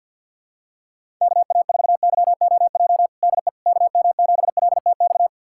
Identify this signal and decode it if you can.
Morse code, heard as a keyed tone, about 50 wpm, 700 Hz. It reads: KA4YOJ DE KM7LTX
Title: USA Callsign DE Callsign 50wpm